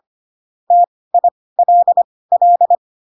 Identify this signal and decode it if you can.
Morse code, heard as a keyed tone, about 25 wpm, spaced as for 18 wpm, 700 Hz. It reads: TILL
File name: Pinocchio / Pinocchio ch4 12 25 ik1hge